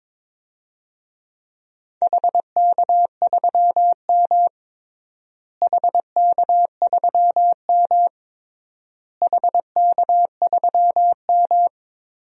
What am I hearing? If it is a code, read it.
Morse code, heard as a keyed tone, about 21 wpm, 700 Hz. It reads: HK3M HK3M HK3M
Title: DXCC-100 Call Signs - 4 Characters - Repeated 3x - 22wpm